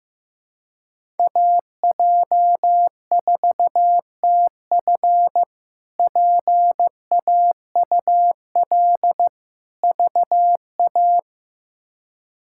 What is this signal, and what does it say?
Morse code, heard as a keyed tone, about 16 wpm, 700 Hz. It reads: AJ4TF PAUL VA
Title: K1USN SST QSO Practice - 15wpm